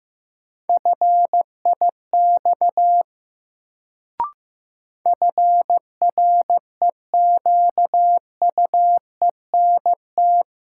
Morse code, heard as a keyed tone, about 15 wpm, 700 Hz. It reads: FIX FREQUENT
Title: Sentences from Top 1400 Words - New Word Drill - 15wpm